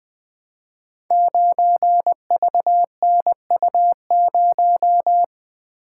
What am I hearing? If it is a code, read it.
Morse code, heard as a keyed tone, about 20 wpm, 700 Hz. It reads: 9VNU0
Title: Code Groups - 5 Letter:Numbers - 20wpm